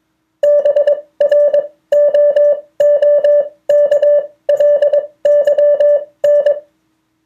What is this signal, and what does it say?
Morse code, heard as a keyed tone, about 21 wpm, spaced as for 17 wpm, 580 Hz. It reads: BROOKLYN